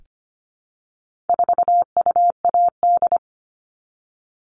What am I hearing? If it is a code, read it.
Morse code, heard as a keyed tone, about 25 wpm, 700 Hz. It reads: 4UAD
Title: All Cut Numbers - 4-Digit 25wpm